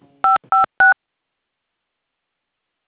DTMF, heard as telephone touch tones, tones of about 140 ms. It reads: 556